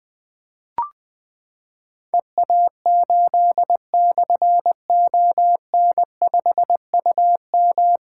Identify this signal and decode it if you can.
Morse code, heard as a keyed tone, about 21 wpm, 700 Hz. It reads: EA8/ON5UM
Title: Difficult Call Signs 20wpm